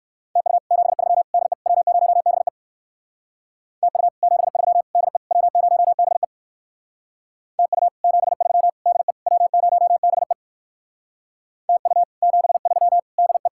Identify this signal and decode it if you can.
Morse code, heard as a keyed tone, about 58 wpm, 700 Hz. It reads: TU 73 DE W0BE TU 73 DE W0BE TU 73 DE W0BE TU 73 DE